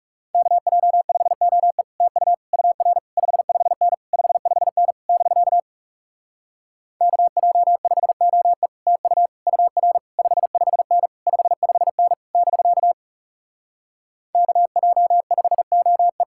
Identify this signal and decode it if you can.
Morse code, heard as a keyed tone, about 40 wpm, 700 Hz. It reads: KJ5OE TU UR 55N 55N <BK> KJ5OE TU UR 55N 55N <BK> KJ5OE